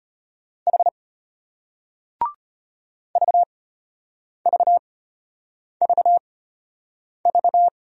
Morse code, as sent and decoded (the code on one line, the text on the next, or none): .... / ...- ...- ...- ...-
H VVVV